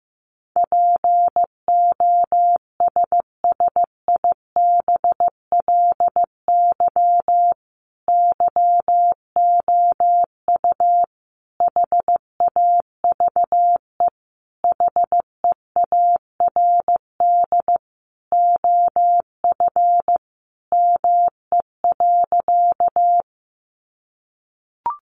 Morse code, heard as a keyed tone, about 15 wpm, 700 Hz. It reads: POSSIBLY YOU HAVE HEARD OF ME.
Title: Sentences from Top 1300 Words 15wpm